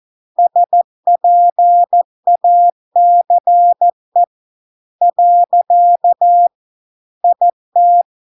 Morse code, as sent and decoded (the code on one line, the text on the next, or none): ... .--. .- -.-. / .-.-.- / .. -
SPACE . IT